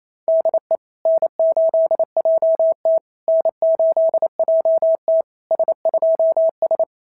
Morse code, heard as a keyed tone, about 27 wpm, 650 Hz. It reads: DE N8JT N8JT S2S
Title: SOTA QSO Practice 28wpm